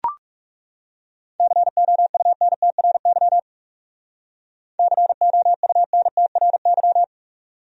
Morse code, heard as a keyed tone, about 38 wpm, 700 Hz. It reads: COUNTRY COUNTRY